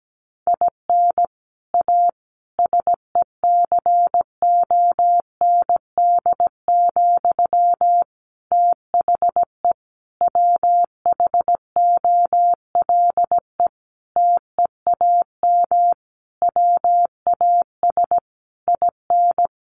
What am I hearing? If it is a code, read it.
Morse code, heard as a keyed tone, about 17 wpm, 700 Hz. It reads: IN A SECOND, THE WHOLE TEAM WAS IN
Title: Sentences from Top 1500 Words 17wpm